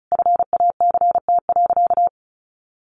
Morse code, heard as a keyed tone, about 34 wpm, 700 Hz. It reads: FACT.
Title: Sentences from Top 1200 Words 35wpm